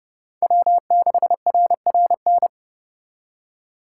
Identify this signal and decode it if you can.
Morse code, heard as a keyed tone, about 36 wpm, 700 Hz. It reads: W6RRN
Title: Speed Racing - USA 1x3 Call Signs 30wpm